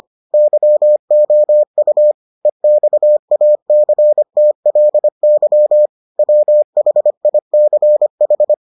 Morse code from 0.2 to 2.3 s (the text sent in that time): YOU